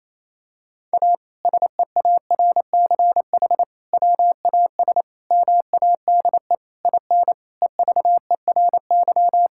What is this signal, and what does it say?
Morse code, heard as a keyed tone, about 27 wpm, 700 Hz. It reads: A SEARCH WAS MADE IN EVERY
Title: Sentences from Top 1000 Words - Review Part 5 - 28wpm